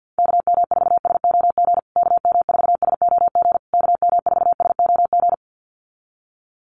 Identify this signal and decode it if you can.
Morse code, heard as a keyed tone, about 48 wpm, 700 Hz. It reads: KM4SOG KM4SOG KM4SOG
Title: USA Call Signs Repeated 3x 50wpm